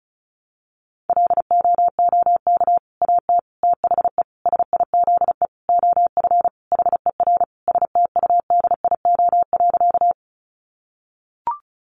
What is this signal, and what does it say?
Morse code, heard as a keyed tone, about 34 wpm, 700 Hz. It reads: LOOK AT THE SIZE OF HER STUDIO.
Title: Sentences from Top 1200 Words 35wpm